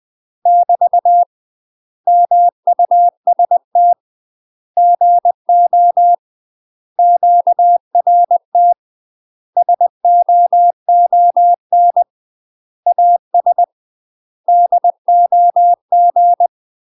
Morse code, heard as a keyed tone, about 20 wpm, 700 Hz. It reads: <BT> MUST GO QRT SOON AS DOG
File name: UK MASTER QSOs TRAINING / 20WPM / UK MASTER QSOs 0014